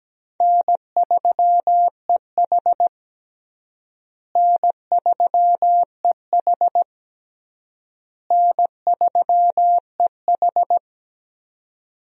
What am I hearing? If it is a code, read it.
Morse code, heard as a keyed tone, about 17 wpm, 700 Hz. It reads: N3EH N3EH N3EH
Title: USA Call Signs - 4 Characters - Repeated 3x - 17wpm